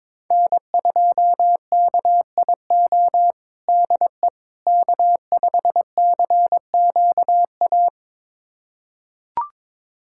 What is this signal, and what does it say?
Morse code, heard as a keyed tone, about 22 wpm, 700 Hz. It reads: N2KIO DE K5CQA
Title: USA Callsign DE Callsign - 5 Characters - 22wpm